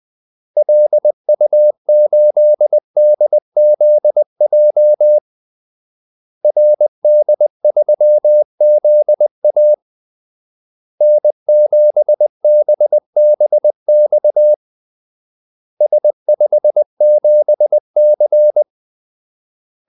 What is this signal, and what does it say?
Morse code, heard as a keyed tone, about 20 wpm, 600 Hz. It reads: LU8DZJ RD3ZA N7BBX S57C
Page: Kaldesignaler 20 wpm